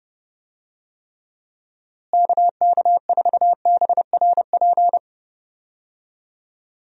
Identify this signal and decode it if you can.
Morse code, heard as a keyed tone, about 30 wpm, 700 Hz. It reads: KK4BRP
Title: USA Call Signs - 6 Characters - Repeated 2x - 30wpm